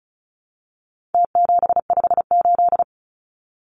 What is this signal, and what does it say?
Morse code, heard as a keyed tone, about 34 wpm, 700 Hz. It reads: T758
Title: Numbers - Contest Style - 4-Digit 35wpm